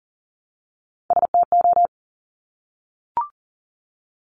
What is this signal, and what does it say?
Morse code, heard as a keyed tone, about 42 wpm, 700 Hz. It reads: STO
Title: Top Three Letter Combinations in Words 40wpm